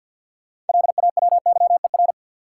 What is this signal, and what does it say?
Morse code, heard as a keyed tone, about 50 wpm, 700 Hz. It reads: LAWYER
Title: Six Letter Words - 50wpm